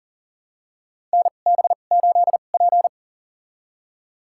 Morse code, heard as a keyed tone, about 40 wpm, spaced as for 29 wpm, 700 Hz. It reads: NB8P
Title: USA Call Signs - 2x Character Spacing - 40wpm